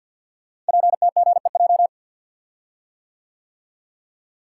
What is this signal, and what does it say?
Morse code, heard as a keyed tone, about 50 wpm, 700 Hz. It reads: PTGEJ
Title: Code Groups - 5 Letters - 50wpm